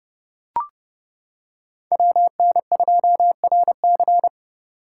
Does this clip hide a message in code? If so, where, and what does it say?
Yes: Morse code from 1.9 to 4.3 s, heard as a keyed tone, about 28 wpm, 700 Hz. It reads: WN2RC